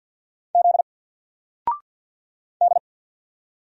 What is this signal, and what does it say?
Morse code, heard as a keyed tone, about 50 wpm, 700 Hz. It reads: Z D
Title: Single Character (Letter:Number:Punctuation:Prosign) 50wpm